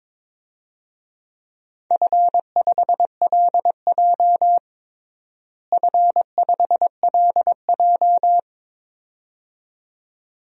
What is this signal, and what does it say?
Morse code, heard as a keyed tone, about 21 wpm, 700 Hz. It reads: F5LJ F5LJ
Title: DXCC-100 Call Signs - 4 Characters - Repeated 2x - 22wpm